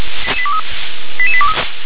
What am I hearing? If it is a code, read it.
Five-tone selective call (ZVEI), heard as quick runs of five tones, tones of about 70 ms. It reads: 88823